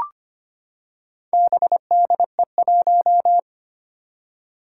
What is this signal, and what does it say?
Morse code, heard as a keyed tone, about 25 wpm, 700 Hz. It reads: BDE1
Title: Four Challenging Letter-Number Combinations 25wpm